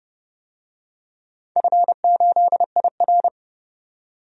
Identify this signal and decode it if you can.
Morse code, heard as a keyed tone, about 30 wpm, 700 Hz. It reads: F8IR
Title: DXCC-100 Call Signs - 4 Characters - 30wpm